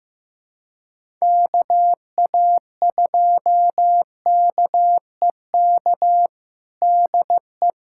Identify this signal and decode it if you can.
Morse code, heard as a keyed tone, about 15 wpm, 700 Hz. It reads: KA2KEK DE
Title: USA Callsign DE Callsign - 6 Characters - 15wpm